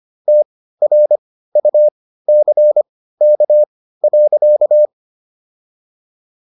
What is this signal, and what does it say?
Morse code, heard as a keyed tone, about 25 wpm, spaced as for 15 wpm, 600 Hz. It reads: TRUCK.